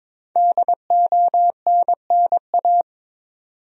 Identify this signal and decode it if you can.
Morse code, heard as a keyed tone, about 22 wpm, 700 Hz. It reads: DONNA
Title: USA Common Names - 22wpm